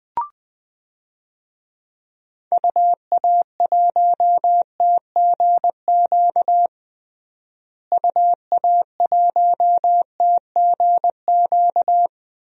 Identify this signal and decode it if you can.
Morse code, heard as a keyed tone, about 21 wpm, 700 Hz. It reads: UA1TGQ UA1TGQ
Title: DXCC-100 Call Signs - 6 Characters - Repeated 2x - 20wpm